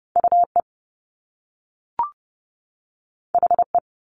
Morse code, heard as a keyed tone, about 30 wpm, 700 Hz. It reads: UE HE
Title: Top Two Letter Combinations in Words 30wpm